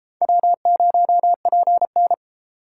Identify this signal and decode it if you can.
Morse code, heard as a keyed tone, about 33 wpm, 700 Hz. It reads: W0PN